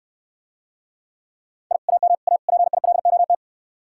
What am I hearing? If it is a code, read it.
Morse code, heard as a keyed tone, about 69 wpm, 700 Hz. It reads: I AM A PILOT